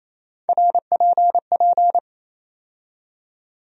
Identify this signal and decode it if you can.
Morse code, heard as a keyed tone, about 28 wpm, 700 Hz. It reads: RPP
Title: USA Call Sign Postfixes - 3 Letters 28wpm